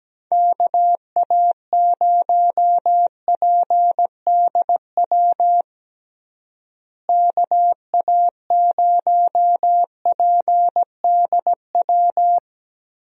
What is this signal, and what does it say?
Morse code, heard as a keyed tone, about 17 wpm, 700 Hz. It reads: KA0PDW KA0PDW
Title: USA Call Signs - 6 Characters - Repeated 3x - 17wpm